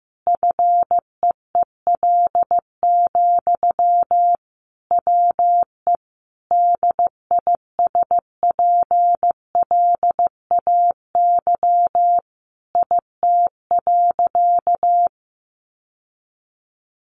Morse code, heard as a keyed tone, about 15 wpm, 700 Hz. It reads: FEEL, WE DISPLAY IT.
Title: Sentences from Top 1200 Words 15wpm